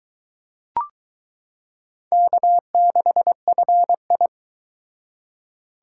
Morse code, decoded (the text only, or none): K6FI